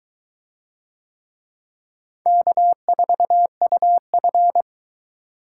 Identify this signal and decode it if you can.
Morse code, heard as a keyed tone, about 22 wpm, 700 Hz. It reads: K4UF